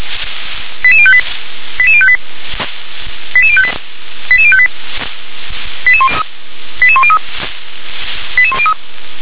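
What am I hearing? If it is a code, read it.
Five-tone selective call (ZVEI), heard as quick runs of five tones, tones of about 70 ms. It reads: 88058 88058 88058 88058 88103 88103
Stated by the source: ZVEI five-tone selective call received on SDR.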